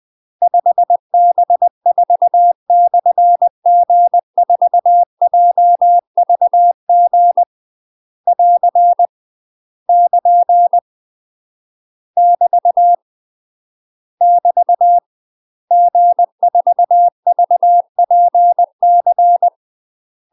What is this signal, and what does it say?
Morse code, heard as a keyed tone, about 20 wpm, 700 Hz. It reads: 5B4/G4JVG <AR> <KN> <BT> <BT> G4VPC
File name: UK MASTER QSOs TRAINING / 20WPM / UK MASTER QSOs 0035